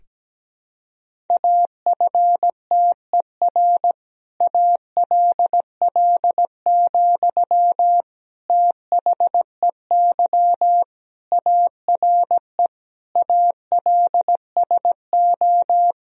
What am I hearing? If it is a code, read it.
Morse code, heard as a keyed tone, about 17 wpm, 700 Hz. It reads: AFTER ALL, THEY ARE ALSO
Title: Sentences from Top 1000 Words - Review Part 1 - 17wpm